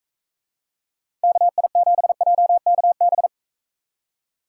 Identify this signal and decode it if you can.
Morse code, heard as a keyed tone, about 43 wpm, 700 Hz. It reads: KI7JKB